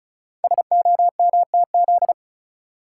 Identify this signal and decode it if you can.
Morse code, heard as a keyed tone, about 33 wpm, 690 Hz. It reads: SOMTZ
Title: Code Groups - 5 Letter:Numbers - 35wpm